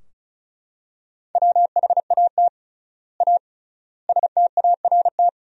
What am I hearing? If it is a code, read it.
Morse code, heard as a keyed tone, about 34 wpm, 700 Hz. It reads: WHAT A START
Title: 3x Word-Spacing - Sentences from Top 300 Words - 35wpm